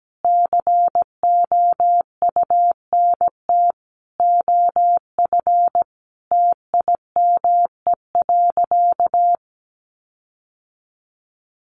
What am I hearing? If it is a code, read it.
Morse code, heard as a keyed tone, about 17 wpm, 700 Hz. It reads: COUNT OF TIME.
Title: Sentences from Top 1500 Words 17wpm